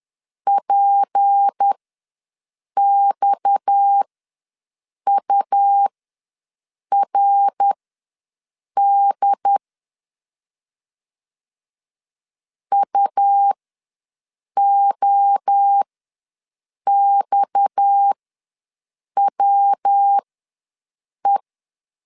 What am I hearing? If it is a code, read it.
Morse code, heard as a keyed tone, about 10 wpm, spaced as for 6 wpm, 800 Hz. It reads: PXURD UOXWE
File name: LEZIONI / 6 PKX0 / 6 PKX ws 10wpm 3706